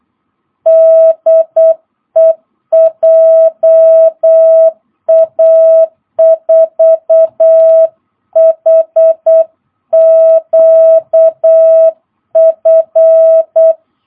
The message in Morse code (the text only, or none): DE JA4HQF